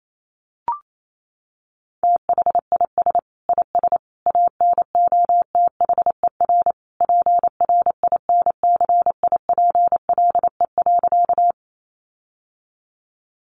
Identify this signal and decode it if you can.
Morse code, heard as a keyed tone, about 28 wpm, 700 Hz. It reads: THIS IS ANOTHER PRINCIPLE.
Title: Sentences from Top 1400 Words 28wpm